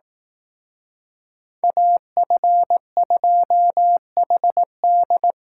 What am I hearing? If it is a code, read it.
Morse code, heard as a keyed tone, about 18 wpm, 700 Hz. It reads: AF2HD
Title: Speed Racing - USA 2x2 Call Signs 15wpm